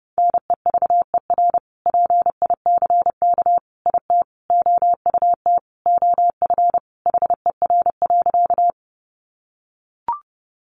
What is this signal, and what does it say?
Morse code, heard as a keyed tone, about 30 wpm, 700 Hz. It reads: NEVER PICK IT OUT OF HER.
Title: Sentences from Top 1000 Words 30wpm